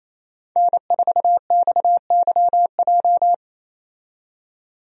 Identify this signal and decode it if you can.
Morse code, heard as a keyed tone, about 28 wpm, 700 Hz. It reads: N4XYJ
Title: USA Call Signs - 5 Characters - 28wpm